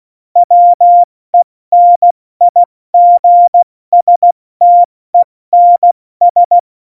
Morse code, heard as a keyed tone, about 16 wpm, 700 Hz. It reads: WENIGSTENS
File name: Morsekurs / Witze / Witz 20. August 2019, 70 F80